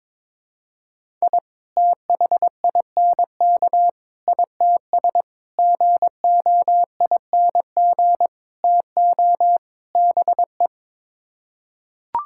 Morse code, heard as a keyed tone, about 22 wpm, 700 Hz. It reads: I THINK ITS GOING TO BE
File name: practice / 22wpm / Sets of 6 Words - 22wpm